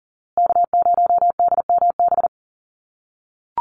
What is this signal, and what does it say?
Morse code, heard as a keyed tone, about 40 wpm, 700 Hz. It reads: K0DMB
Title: USA Call Signs Repeated 2x 40wpm